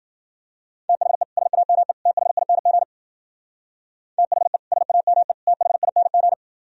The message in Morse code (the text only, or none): THE SAME THING THE SAME THING